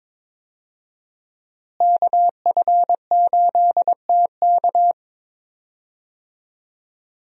Morse code, heard as a keyed tone, about 21 wpm, 700 Hz. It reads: KF8TK
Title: USA Call Signs - 5 Characters - Repeated 3x - 22wpm